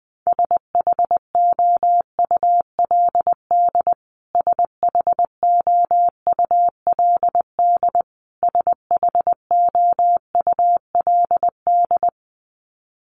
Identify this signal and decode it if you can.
Morse code, heard as a keyed tone, about 20 wpm, 700 Hz. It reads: SHOULD SHOULD SHOULD
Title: Top 500 Words Repeated 20wpm